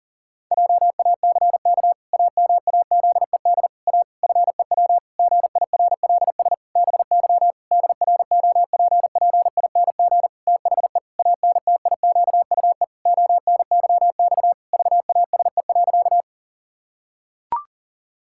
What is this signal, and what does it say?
Morse code, heard as a keyed tone, about 37 wpm, 700 Hz. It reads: JACK AMAZED A FEW GIRLS BY DROPPING THE ANTIQUE ONYX VASE.
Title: Pangram Challenge v2 - 40wpm